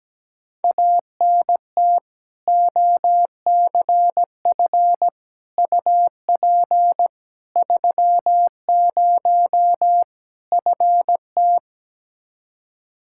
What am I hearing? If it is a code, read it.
Morse code, heard as a keyed tone, about 17 wpm, 700 Hz. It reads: ANT OCF UP 30 FT
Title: Practice QSO - Exchange 2 - Info - 17wpm